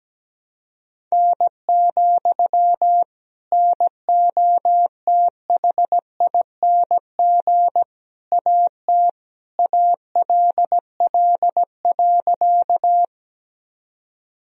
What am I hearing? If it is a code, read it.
Morse code, heard as a keyed tone, about 17 wpm, 700 Hz. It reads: N, NOTHING AT ALL.
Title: Sentences from Top 500 Words and Abbreviations 17wpm